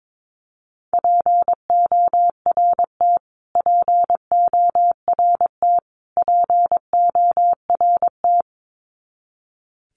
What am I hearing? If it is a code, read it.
Morse code, heard as a keyed tone, about 22 wpm, 700 Hz. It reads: PORT PORT PORT
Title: Top 300 Words Repeated 22wpm